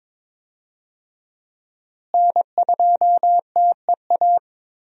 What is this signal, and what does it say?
Morse code, heard as a keyed tone, about 22 wpm, 700 Hz. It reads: N2TEA
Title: USA Call Signs - 5 Characters - Repeated 2x - 22wpm